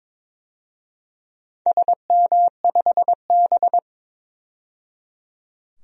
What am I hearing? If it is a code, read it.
Morse code, heard as a keyed tone, about 22 wpm, 700 Hz. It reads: SM5B
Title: DXCC-100 Call Signs - 4 Characters - 22wpm